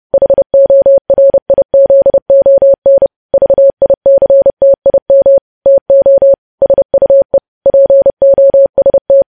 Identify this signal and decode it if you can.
Morse code, heard as a keyed tone, about 30 wpm, 550 Hz. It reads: HORIZON VICTIM TO SUE POST
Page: news headlines in morse code at 30 wpm